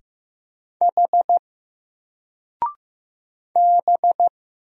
Morse code, heard as a keyed tone, about 15 wpm, 700 Hz. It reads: H B
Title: Single Character (Letter:Number:Punctuation:Prosign) 15wpm